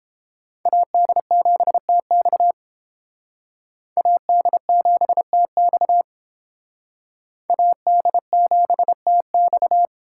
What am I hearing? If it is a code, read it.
Morse code, heard as a keyed tone, about 28 wpm, 700 Hz. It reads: AD7TX AD7TX AD7TX